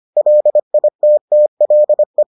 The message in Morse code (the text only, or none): LITTLE